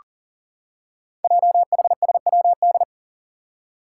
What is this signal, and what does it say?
Morse code, heard as a keyed tone, about 42 wpm, 700 Hz. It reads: JHSWD
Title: Code Groups - 5 Letters - 40wpm